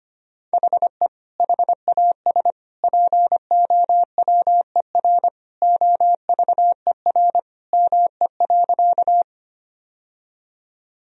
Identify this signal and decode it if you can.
Morse code, heard as a keyed tone, about 25 wpm, 700 Hz. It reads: HE HAS POWER OVER ME.
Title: Sentences from Top 1000 Words - Review Part 2 - 25wpm